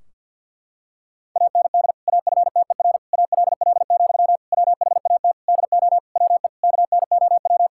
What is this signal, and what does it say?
Morse code, heard as a keyed tone, about 50 wpm, 700 Hz. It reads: AND AFTER ALL, WHAT DO WE KNOW
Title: Sentences from Top 1600 Words 50wpm